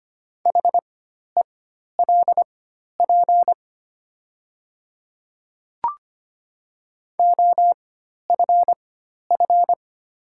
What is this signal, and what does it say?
Morse code, heard as a keyed tone, about 25 wpm, spaced as for 12 wpm, 700 Hz. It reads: HELP OFF
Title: 4x Character Spacing - Top 200 Words - 25wpm